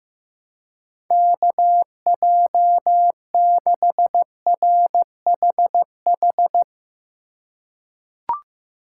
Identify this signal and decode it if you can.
Morse code, heard as a keyed tone, about 15 wpm, 700 Hz. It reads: KJ6RHH
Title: USA Call Signs - 6 Characters - 3x Character Spacing - 15wpm